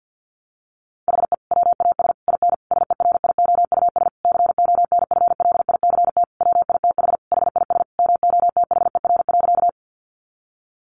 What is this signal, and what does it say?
Morse code, heard as a keyed tone, about 50 wpm, 700 Hz. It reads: HE WAS IN SERIOUS CONFLICT WITH HIS MOTHER.
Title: Sentences from Top 1300 Words 50wpm